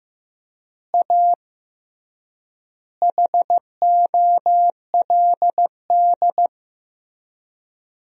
Morse code, heard as a keyed tone, about 15 wpm, 700 Hz. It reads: A HOLD
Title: 3x Word-Spacing - Sentences from Top 400 Words - 15wpm